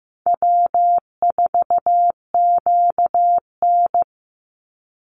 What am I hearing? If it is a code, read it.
Morse code, heard as a keyed tone, about 15 wpm, 700 Hz. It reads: W4QN
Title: Easy Call Signs Repeated 3x - 15wpm